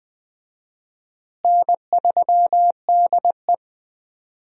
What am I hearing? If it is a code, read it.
Morse code, heard as a keyed tone, about 20 wpm, 700 Hz. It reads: N3DE
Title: USA Call Signs - 4 Characters - Repeated 3x - 20wpm